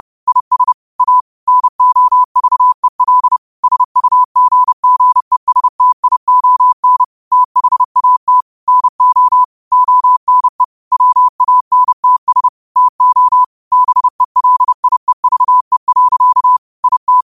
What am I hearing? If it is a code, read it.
Morse code, heard as a keyed tone, about 30 wpm, 1000 Hz. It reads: IS A NOVEL SUGGESTION THAT NO ONE WANTS TO BELIEVE. IT
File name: qotdfiles / 30wpm